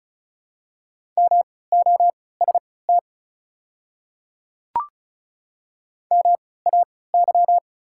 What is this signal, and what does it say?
Morse code, heard as a keyed tone, about 35 wpm, spaced as for 20 wpm, 690 Hz. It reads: MOST MAY
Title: 3x Character Spacing - Top 100 Words - 35wpm